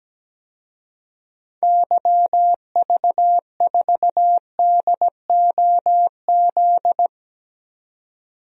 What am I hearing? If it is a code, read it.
Morse code, heard as a keyed tone, about 17 wpm, 700 Hz. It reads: YV4DOZ
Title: DXCC-100 Call Signs - 6 Characters - 17wpm